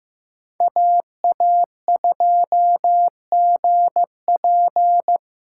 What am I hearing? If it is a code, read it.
Morse code, heard as a keyed tone, about 15 wpm, 700 Hz. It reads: AA2GP
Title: USA Call Signs - 8x Character Spacing - 15wpm